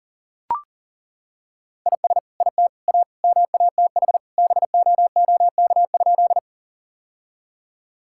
Morse code, heard as a keyed tone, about 37 wpm, 700 Hz. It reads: IS IT A MATH BOOK?
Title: Sentences from Top 1000 Words - Review Part 6 - 40wpm